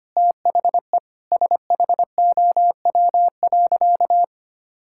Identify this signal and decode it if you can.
Morse code, heard as a keyed tone, about 25 wpm, 700 Hz. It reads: THE SHOW.
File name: practice / 25wpm / Sentences with 4-Word Phrases - 25wpm